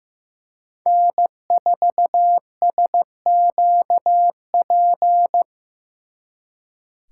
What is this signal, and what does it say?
Morse code, heard as a keyed tone, about 18 wpm, 700 Hz. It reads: N4SQP
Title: Speed Racing - USA 1x3 Call Signs 15wpm